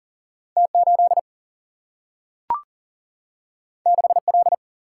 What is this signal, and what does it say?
Morse code, heard as a keyed tone, about 40 wpm, 700 Hz. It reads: T 8 6 L
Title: Single-Letter Single-Number 40wpm